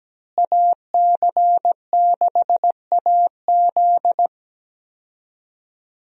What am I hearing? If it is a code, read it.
Morse code, heard as a keyed tone, about 17 wpm, 700 Hz. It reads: AC6AZ
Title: USA Call Signs - 5 Characters - Repeated 3x - 17wpm